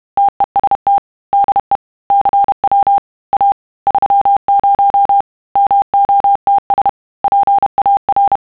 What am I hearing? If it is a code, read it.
Morse code, heard as a keyed tone, about 31 wpm, 800 Hz. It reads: TEST DE CW A 30 MOTS PAR
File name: sons / TEST30